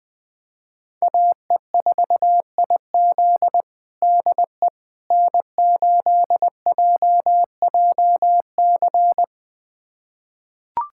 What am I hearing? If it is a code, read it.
Morse code, heard as a keyed tone, about 20 wpm, 700 Hz. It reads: AE4IZ DE N8JJC
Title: USA Callsign DE Callsign - 5 Characters - 20wpm